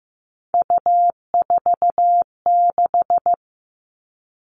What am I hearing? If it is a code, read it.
Morse code, heard as a keyed tone, about 15 wpm, 700 Hz. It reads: U46
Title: All Cut Numbers - 3-Digit 15wpm